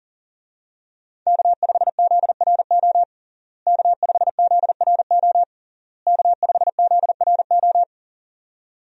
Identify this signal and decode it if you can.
Morse code, heard as a keyed tone, about 40 wpm, 700 Hz. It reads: K5ZRO K5ZRO K5ZRO
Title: USA Call Signs - 5 Characters - Repeated 3x - 40wpm